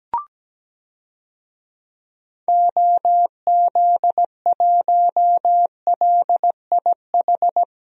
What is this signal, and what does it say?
Morse code, heard as a keyed tone, about 17 wpm, 700 Hz. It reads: OZ1LIH
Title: DXCC-100 Call Signs - 6 Characters - 17wpm